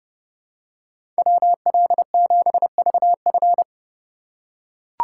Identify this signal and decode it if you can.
Morse code, heard as a keyed tone, about 30 wpm, 700 Hz. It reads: WL7VF
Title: USA Call Signs - 5 Characters - 3x Character Spacing - 30wpm